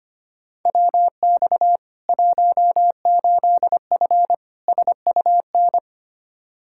Morse code, heard as a keyed tone, about 25 wpm, 700 Hz. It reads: WX 18F SUN
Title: Practice QSO - Exchange 2 - Info - 25wpm